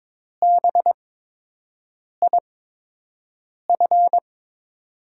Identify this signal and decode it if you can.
Morse code, heard as a keyed tone, about 22 wpm, spaced as for 6 wpm, 700 Hz. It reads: BIF
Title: USA Call Signs - 6 Characters - 8x Character Spacing - 22wpm